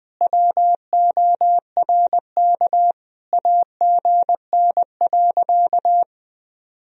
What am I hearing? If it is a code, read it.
Morse code, heard as a keyed tone, about 21 wpm, 700 Hz. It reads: WORK AGN.
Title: Sentences from Top 200 Words and Abbreviations 20wpm